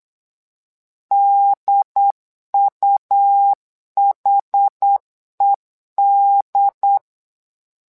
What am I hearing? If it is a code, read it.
Morse code, heard as a keyed tone, about 8 wpm, 800 Hz. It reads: DUHED
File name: LEZIONI / 2 UHDN / 2 UHD 8wpm 2258